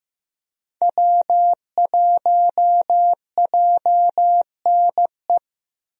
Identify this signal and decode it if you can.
Morse code, heard as a keyed tone, about 15 wpm, 700 Hz. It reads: W1JNE
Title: USA Call Signs - 5 Characters - Repeated 3x - 15wpm